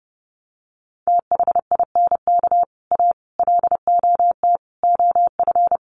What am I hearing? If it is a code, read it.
Morse code, heard as a keyed tone, about 30 wpm, 700 Hz. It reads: THINK A LOT OF
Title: Sets of 5 Words 30wpm